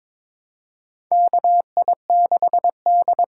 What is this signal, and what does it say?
Morse code, heard as a keyed tone, about 22 wpm, 700 Hz. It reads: KI6D
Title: USA Call Signs - 6 Characters - 8x Character Spacing - 22wpm